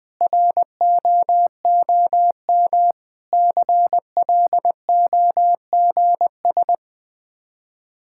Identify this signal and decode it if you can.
Morse code, heard as a keyed tone, about 20 wpm, 700 Hz. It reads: ROOM CLOGS
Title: Difficult Words - Speed-Racing with Hints - 20wpm